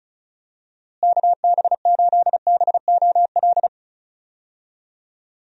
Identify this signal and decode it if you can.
Morse code, heard as a keyed tone, about 36 wpm, 700 Hz. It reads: KB8BOL
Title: USA Call Signs - 6 Characters - Repeated 2x - 35wpm